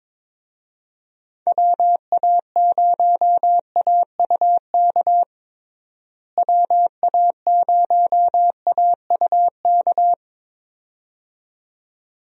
Morse code, heard as a keyed tone, about 22 wpm, 700 Hz. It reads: WA0AUK WA0AUK
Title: USA Call Signs - 6 Characters - Repeated 2x - 22wpm